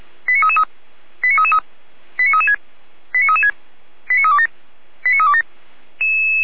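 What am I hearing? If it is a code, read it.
Five-tone selective call (ZVEI), heard as quick runs of five tones, tones of about 70 ms. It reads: 89333 89333 89336 89336 89327 89327
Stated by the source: ZVEI five-tone selective call received by SDR.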